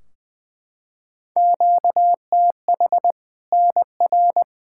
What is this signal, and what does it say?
Morse code, heard as a keyed tone, about 20 wpm, 700 Hz. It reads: QTH NR
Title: Practice QSO - Exchange 1 - Beginning - 20wpm